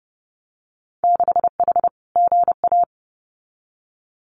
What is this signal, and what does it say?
Morse code, heard as a keyed tone, about 30 wpm, 700 Hz. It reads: 6H GA
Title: Winter Field Day Exchanges Encore 30wpm